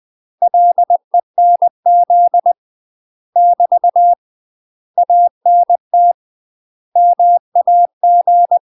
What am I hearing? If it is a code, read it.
Morse code, heard as a keyed tone, about 20 wpm, 700 Hz. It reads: LENZ <BT> ANT MAG
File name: UK MASTER QSOs TRAINING/20WPM